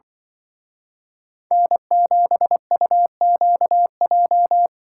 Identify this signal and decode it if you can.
Morse code, heard as a keyed tone, about 25 wpm, 700 Hz. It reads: N7UQJ